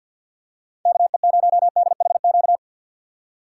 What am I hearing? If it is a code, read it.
Morse code, heard as a keyed tone, about 52 wpm, 700 Hz. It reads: KE0DHQ
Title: USA Call Signs - 6 Characters - 3x Character Spacing - 50wpm